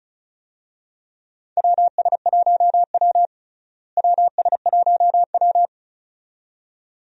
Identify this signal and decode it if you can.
Morse code, heard as a keyed tone, about 34 wpm, 700 Hz. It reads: WS1W WS1W
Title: USA Call Signs - 4 Characters - Repeated 2x - 35wpm